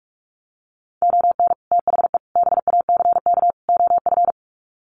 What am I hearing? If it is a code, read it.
Morse code, heard as a keyed tone, about 44 wpm, 700 Hz. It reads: ON THE BACK OF
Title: Sets of 4 Words 45wpm